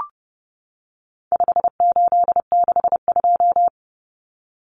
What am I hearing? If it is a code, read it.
Morse code, heard as a keyed tone, about 30 wpm, 700 Hz. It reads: 5862
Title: Common Cut Numbers - 4-Digit 30wpm